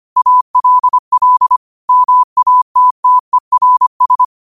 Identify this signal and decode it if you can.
Morse code, heard as a keyed tone, about 25 wpm, 1000 Hz. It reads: ALL MATTERS
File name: qotdfiles / 25wpm